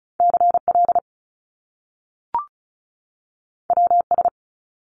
Morse code, heard as a keyed tone, about 36 wpm, 700 Hz. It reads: CL WS
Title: Top Two Letter Combinations in Words 35wpm